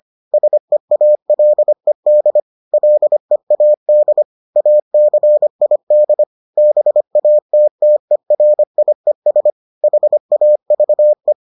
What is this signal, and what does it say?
Morse code, heard as a keyed tone, about 25 wpm, 600 Hz. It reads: SEALED LEAD ACID BATTERIES HAVE